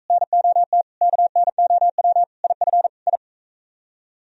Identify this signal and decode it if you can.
Morse code, heard as a keyed tone, about 42 wpm, 700 Hz. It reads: NOT KNOW IF I